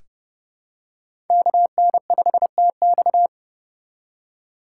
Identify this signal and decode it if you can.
Morse code, heard as a keyed tone, about 30 wpm, 700 Hz. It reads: KN5TX
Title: K1USN SST QSO Practice - 30wpm